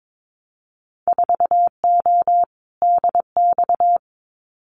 Morse code, heard as a keyed tone, about 22 wpm, 700 Hz. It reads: 4O DX
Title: Winter Field Day Exchanges Encore 22wpm